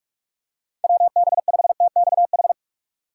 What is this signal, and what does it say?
Morse code, heard as a keyed tone, about 43 wpm, 700 Hz. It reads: WB5TXH
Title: USA Call Signs - 6 Characters - 3x Character Spacing - 45wpm